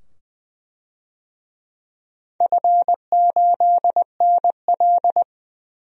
Morse code, heard as a keyed tone, about 20 wpm, 700 Hz. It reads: F8NL
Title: DXCC-100 Call Signs - 4 Characters - 20wpm